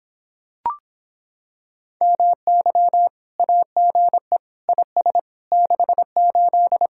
Morse code, heard as a keyed tone, about 26 wpm, 700 Hz. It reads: MY AGE IS 68